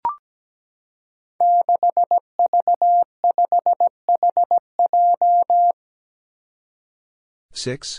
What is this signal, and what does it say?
Morse code, heard as a keyed tone, about 17 wpm, 700 Hz. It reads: 6V5HJ
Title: Five Challenging Letter-Number Combinations 17wpm